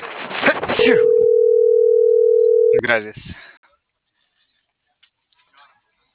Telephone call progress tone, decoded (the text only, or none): ringback tone